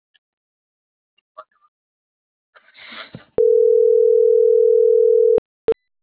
Ringback tone (North America), 440 + 480 Hz.